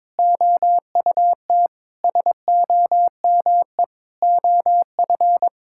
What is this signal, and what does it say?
Morse code, heard as a keyed tone, about 22 wpm, 700 Hz. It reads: OUT SOME OF
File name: Sets of 9 Words - 22wpm